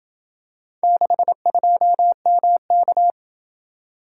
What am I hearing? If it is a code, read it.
Morse code, heard as a keyed tone, about 27 wpm, 700 Hz. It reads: 62MK